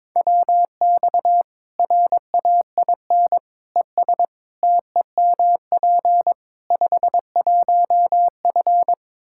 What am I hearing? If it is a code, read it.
Morse code, heard as a keyed tone, about 22 wpm, 700 Hz. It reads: WX RAIN ES TEMP 51F
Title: Practice QSO - Exchange 2 - Info - 22wpm